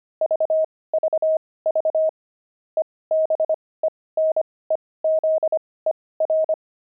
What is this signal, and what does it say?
Morse code, heard as a keyed tone, about 25 wpm, spaced as for 18 wpm, 630 Hz. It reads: VVV EBENEZER
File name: pangram-fs-18-cs-25-aardvark